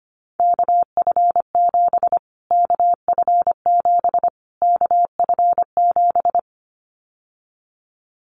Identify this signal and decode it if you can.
Morse code, heard as a keyed tone, about 25 wpm, 700 Hz. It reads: KF7 KF7 KF7
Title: Callsign Prefixes for Top 25 DXCC - Repeated 3x - 25wpm